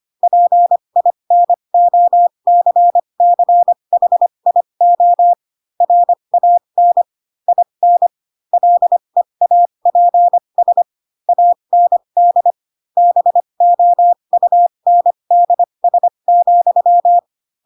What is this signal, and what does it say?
Morse code, heard as a keyed tone, about 25 wpm, spaced as for 22 wpm, 700 Hz. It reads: PINOCCHIO RAN IN LEAPS AND BOUNDS,
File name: Pinocchio / Pinocchio ch3 18 25 ik1hge